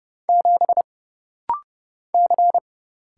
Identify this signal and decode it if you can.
Morse code, heard as a keyed tone, about 30 wpm, 700 Hz. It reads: 7 C
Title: Single Letter-Number Alt - Rapid-Fire 30wpm